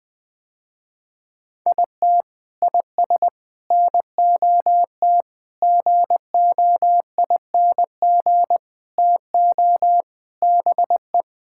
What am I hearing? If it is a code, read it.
Morse code, heard as a keyed tone, about 20 wpm, 700 Hz. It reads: IT IS NOT GOING TO BE
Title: Sentences with 4-Word Phrases - Spoken Beginning - 20wpm